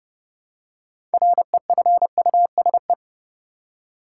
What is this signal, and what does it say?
Morse code, heard as a keyed tone, about 30 wpm, 700 Hz. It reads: REFUSE
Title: Six Letter Words - 30wpm